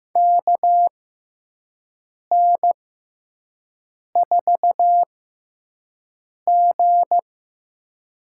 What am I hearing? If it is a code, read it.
Morse code, heard as a keyed tone, about 15 wpm, spaced as for 5 wpm, 700 Hz. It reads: KN4G
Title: USA Call Signs - 5 Characters - 6x Character Spacing - 15wpm